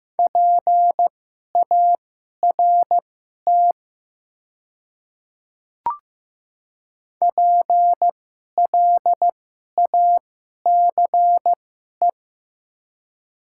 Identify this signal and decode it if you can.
Morse code, heard as a keyed tone, about 15 wpm, spaced as for 11 wpm, 700 Hz. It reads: PART PLACE
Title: 2x Character Spacing - Top 300 Words - 15wpm